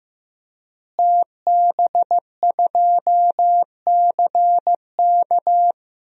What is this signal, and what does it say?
Morse code, heard as a keyed tone, about 15 wpm, 700 Hz. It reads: TB2CK
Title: DXCC-100 Call Signs - 5 Characters - Repeated 2x - 15wpm